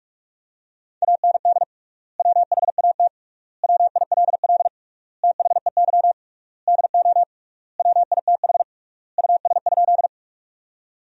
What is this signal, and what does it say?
Morse code, heard as a keyed tone, about 46 wpm, 700 Hz. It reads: AND WHAT WILL THEY DO WITH US?
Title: 3x Word-Spacing - Sentences from Top 200 Words - 45wpm